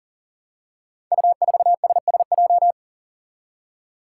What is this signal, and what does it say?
Morse code, heard as a keyed tone, about 39 wpm, 700 Hz. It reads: U4SSJ
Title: Five Challenging Letter-Number Combinations 40wpm